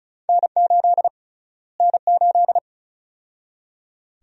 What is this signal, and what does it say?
Morse code, heard as a keyed tone, about 36 wpm, 700 Hz. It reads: N8 N8
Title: USA Call Sign Prefixes - Repeated 2x - 35wpm